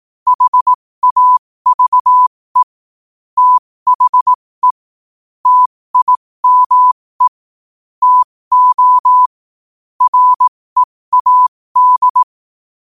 Morse code, heard as a keyed tone, about 18 wpm, spaced as for 15 wpm, 1000 Hz. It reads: HAVE THE TIME TO READ
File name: qotdfiles / 15wpm